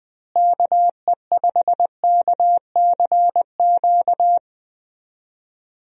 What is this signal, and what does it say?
Morse code, heard as a keyed tone, about 21 wpm, 700 Hz. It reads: KE5KCQ
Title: USA Call Signs - 6 Characters - Repeated 2x - 20wpm